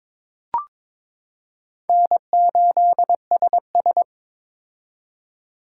Morse code, heard as a keyed tone, about 21 wpm, 700 Hz. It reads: N8SS
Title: K1USN SST QSO Practice - 22wpm